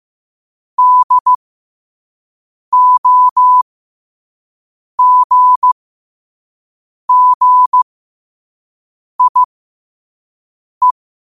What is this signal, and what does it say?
Morse code, heard as a keyed tone, about 15 wpm, spaced as for 5 wpm, 1000 Hz. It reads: DOGGIE
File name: qotdfiles / 5wpm